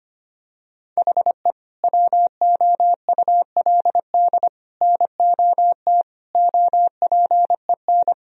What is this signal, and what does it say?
Morse code, heard as a keyed tone, about 25 wpm, 700 Hz. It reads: HE WOULD NOT OPEN
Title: Sentences from Top 1000 Words - Review Part 1 - 25wpm